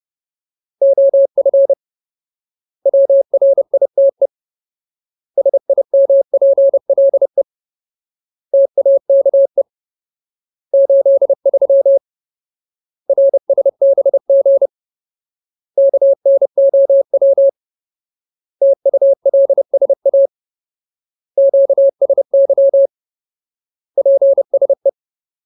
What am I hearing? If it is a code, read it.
Morse code, heard as a keyed tone, about 30 wpm, 550 Hz. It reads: OF WRITE SIMPLE TAKE 83 RSBG KNOW TULSA QSY PSE
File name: Ultimate Word List / 30WPM Ultimate